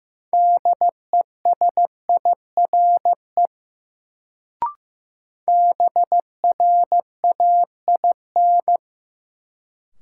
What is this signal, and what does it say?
Morse code, heard as a keyed tone, about 15 wpm, 700 Hz. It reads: DESIRE BRAIN
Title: Sentences from Top 1100 Words - New Word Drill Alt - 15wpm